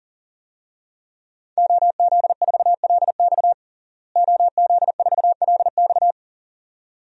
Morse code, heard as a keyed tone, about 40 wpm, 700 Hz. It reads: OZ4LX OZ4LX
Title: DXCC-100 Call Signs - 5 Characters - Repeated 2x - 40wpm